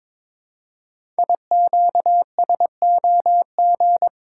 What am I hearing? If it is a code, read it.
Morse code, heard as a keyed tone, about 22 wpm, 700 Hz. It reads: IQSOG
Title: Code Groups - 5 Letters - 22wpm